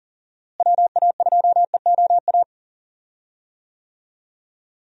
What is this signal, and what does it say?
Morse code, heard as a keyed tone, about 40 wpm, 700 Hz. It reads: WA2EOA
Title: USA Call Signs - 6 Characters - Repeated 3x - 40wpm